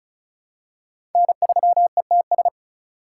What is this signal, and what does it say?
Morse code, heard as a keyed tone, about 36 wpm, 700 Hz. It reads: N3ETS
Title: USA Call Signs - 5 Characters - 4x Character Spacing - 35wpm